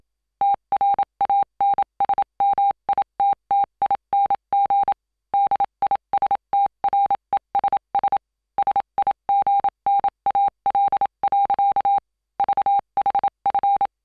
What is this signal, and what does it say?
Morse code, heard as a keyed tone, about 27 wpm, 800 Hz. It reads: TRANSMITTING DISTRESS SIGNAL. VHF